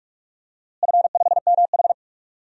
Morse code, heard as a keyed tone, about 46 wpm, 700 Hz. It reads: F5MH
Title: DXCC-100 Call Signs - 4 Characters - Repeated 3x - 45wpm